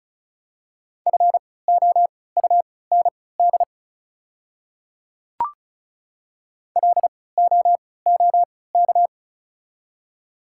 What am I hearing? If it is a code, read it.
Morse code, heard as a keyed tone, about 36 wpm, spaced as for 20 wpm, 700 Hz. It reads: FOUND LOOK
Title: 3x Character Spacing - Top 300 Words - 35wpm